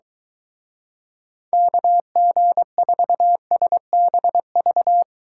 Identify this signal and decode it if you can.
Morse code, heard as a keyed tone, about 23 wpm, 700 Hz. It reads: KG4SBV